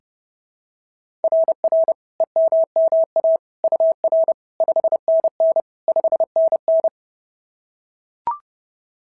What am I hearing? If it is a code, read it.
Morse code, heard as a keyed tone, about 26 wpm, 650 Hz. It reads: RR EMMA UR 5NN 5NN
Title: SOTA QSO Practice 30wpm